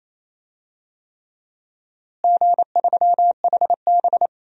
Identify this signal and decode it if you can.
Morse code, heard as a keyed tone, about 27 wpm, 700 Hz. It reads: G3HB
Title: DXCC-100 Call Signs - 4 Characters - Repeated 3x - 28wpm